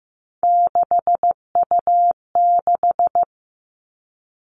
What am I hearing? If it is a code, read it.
Morse code, heard as a keyed tone, about 15 wpm, 700 Hz. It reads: 6U6
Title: All Cut Numbers - 3-Digit 15wpm